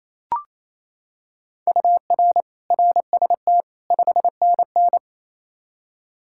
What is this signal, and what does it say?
Morse code, heard as a keyed tone, about 28 wpm, 700 Hz. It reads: UR RST 5NN
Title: Practice QSO - Exchange 1 - Beginning - 28wpm